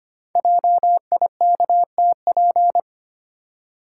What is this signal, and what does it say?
Morse code, heard as a keyed tone, about 25 wpm, 700 Hz. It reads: JIKTP
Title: Code Groups - 5 Letters - 25wpm